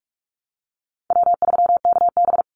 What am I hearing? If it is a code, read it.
Morse code, heard as a keyed tone, about 44 wpm, 700 Hz. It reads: W3KB
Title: Easy Call Signs - 45wpm